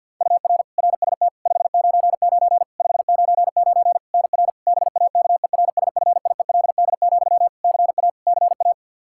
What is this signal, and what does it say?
Morse code, heard as a keyed tone, about 53 wpm, 700 Hz. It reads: UR RST 599 599 NR BAKERSFIELD, CA CA
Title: Practice QSO - Exchange 1 - Beginning - 50wpm